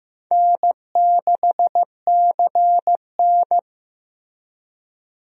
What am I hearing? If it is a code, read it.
Morse code, heard as a keyed tone, about 16 wpm, 700 Hz. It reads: N6CN
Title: USA Call Signs - 4 Characters - Repeated 3x - 15wpm